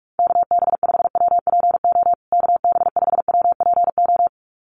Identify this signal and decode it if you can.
Morse code, heard as a keyed tone, about 44 wpm, 700 Hz. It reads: KB5WPO KB5WPO
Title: USA Call Signs Repeated 2x 45wpm